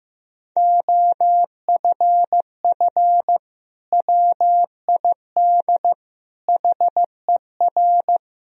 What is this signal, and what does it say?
Morse code, heard as a keyed tone, about 15 wpm, 700 Hz. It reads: OFF WID HER
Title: Sentences from Top 300 Words and Abbreviations 15wpm